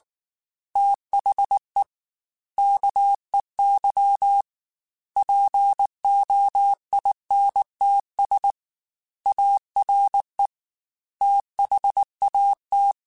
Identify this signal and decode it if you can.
Morse code, heard as a keyed tone, about 19 wpm, 800 Hz. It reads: THE KEY POINTS ARE THAT
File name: Technical Subjects / Technical Subjects-mp3s / Reactance @18